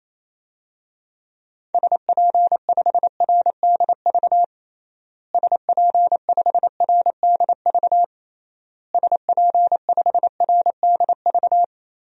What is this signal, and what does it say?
Morse code, heard as a keyed tone, about 28 wpm, 700 Hz. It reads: SP5RDV SP5RDV SP5RDV
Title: DXCC-100 Call Signs - 6 Characters - Repeated 3x - 28wpm